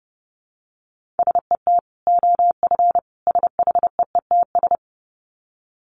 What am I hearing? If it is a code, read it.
Morse code, heard as a keyed tone, about 30 wpm, 700 Hz. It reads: SET OF SHEETS
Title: Dit Challenge 30wpm